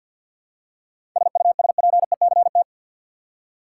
Morse code, heard as a keyed tone, about 49 wpm, 700 Hz. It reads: SUSPECT
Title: Seven Letter Words - 50wpm